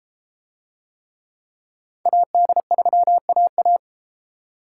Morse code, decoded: AD3AA

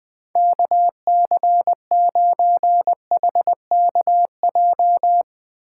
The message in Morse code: KC9HKJ

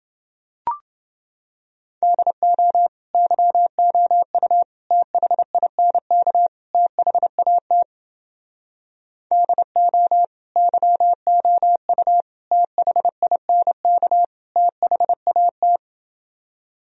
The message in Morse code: DO YOU THINK THAT DO YOU THINK THAT

